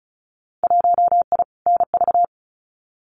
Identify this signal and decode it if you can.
Morse code, heard as a keyed tone, about 36 wpm, 700 Hz. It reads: 1I NV
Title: Winter Field Day Exchanges Encore 35wpm